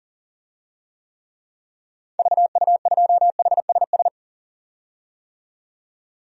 Morse code, heard as a keyed tone, about 37 wpm, 700 Hz. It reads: VU2HSS
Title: DXCC-100 Call Signs - 6 Characters - 40wpm